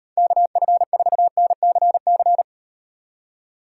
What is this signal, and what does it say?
Morse code, heard as a keyed tone, about 38 wpm, 700 Hz. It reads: KF4NCC